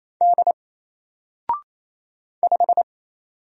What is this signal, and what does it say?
Morse code, heard as a keyed tone, about 27 wpm, 700 Hz. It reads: D 5
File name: Single-Letter-Number-28wpm